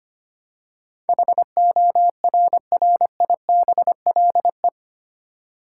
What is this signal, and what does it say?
Morse code, heard as a keyed tone, about 25 wpm, 700 Hz. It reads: HORRIBLE
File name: Eight Letter Words - 25wpm